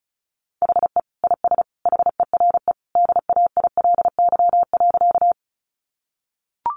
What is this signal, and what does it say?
Morse code, heard as a keyed tone, about 34 wpm, 700 Hz. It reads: HE IS HERE DAILY.
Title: Sentences from Top 1100 Words 35wpm